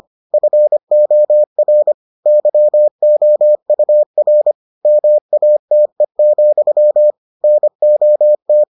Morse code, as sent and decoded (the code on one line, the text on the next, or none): ..-. --- .-. / -.-- --- ..- .-. / -- .- - . --..-- / -. --- -
FOR YOUR MATE, NOT